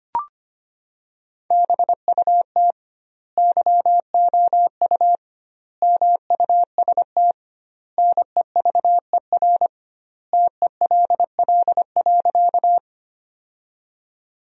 Morse code, heard as a keyed tone, about 25 wpm, 700 Hz. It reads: BUT YOU MUST NEVER TELL.
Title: 2x Word-Spacing - Sentences from Top 300 Words - 25wpm